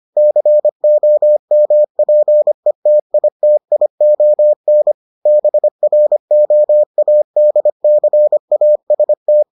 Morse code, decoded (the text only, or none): COMPETITION BROADCAST